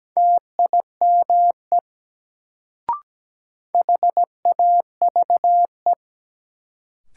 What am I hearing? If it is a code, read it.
Morse code, heard as a keyed tone, about 17 wpm, 700 Hz. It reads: TIME HAVE
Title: Sentences from Top 100 Words - New Word Drill - 17wpm